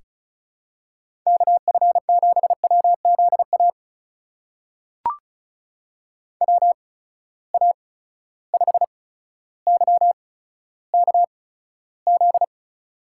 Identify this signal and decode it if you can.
Morse code, heard as a keyed tone, about 36 wpm, spaced as for 15 wpm, 700 Hz. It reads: KF7WZA WA5YKZ
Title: USA Call Signs - 6 Characters - 8x Character Spacing - 35wpm